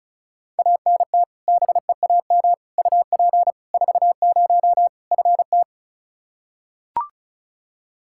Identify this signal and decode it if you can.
Morse code, heard as a keyed tone, about 36 wpm, 700 Hz. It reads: ANT BEAM UP 40 FT
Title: Practice QSO - Exchange 2 - Info - 35wpm